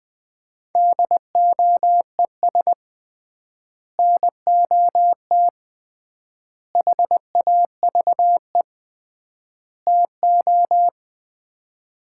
Morse code, heard as a keyed tone, about 21 wpm, 700 Hz. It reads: DOES NOT HAVE TO
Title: 3x Word-Spacing - Sets of 4 Words - 20wpm